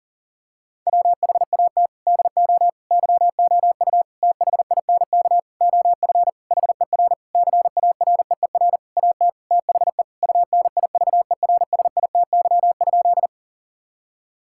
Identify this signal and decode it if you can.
Morse code, heard as a keyed tone, about 40 wpm, 700 Hz. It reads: WHAT DO YOU THINK OF HER CAREER AT THE UNIVERSITY?
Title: Sentences from Top 1000 Words - Review Part 6 - 40wpm